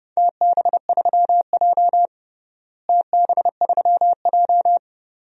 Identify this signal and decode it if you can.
Morse code, heard as a keyed tone, about 30 wpm, 700 Hz. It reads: TB3J TB3J
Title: DXCC-100 Call Signs - 4 Characters - Repeated 2x - 30wpm